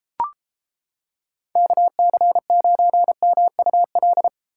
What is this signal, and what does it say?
Morse code, heard as a keyed tone, about 31 wpm, 700 Hz. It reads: KC9MUL